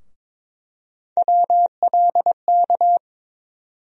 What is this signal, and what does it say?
Morse code, heard as a keyed tone, about 22 wpm, 700 Hz. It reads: WLK
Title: USA Call Sign Postfixes - 3 Letters 22wpm